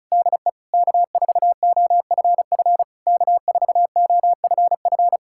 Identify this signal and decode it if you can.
Morse code, heard as a keyed tone, about 40 wpm, 700 Hz. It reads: DE K4OFF K4OFF